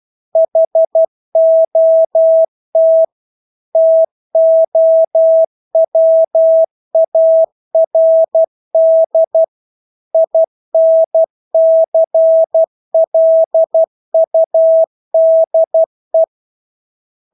Morse code, heard as a keyed tone, about 12 wpm, 650 Hz. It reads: HOT TOWARD INCLUDE
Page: Engelske ord 12 wpm